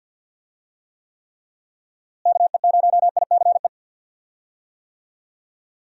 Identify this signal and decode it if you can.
Morse code, heard as a keyed tone, about 48 wpm, 700 Hz. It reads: KE0ICE
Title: USA Call Signs - 6 Characters - Repeated 3x - 50wpm